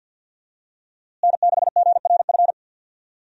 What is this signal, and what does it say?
Morse code, heard as a keyed tone, about 47 wpm, 700 Hz. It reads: N6GRF
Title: USA Call Signs - 5 Characters - Repeated 3x - 50wpm